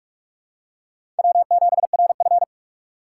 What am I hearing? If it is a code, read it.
Morse code, heard as a keyed tone, about 44 wpm, 700 Hz. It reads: W7RF
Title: USA Call Signs - 4 Characters - 4x Character Spacing - 45wpm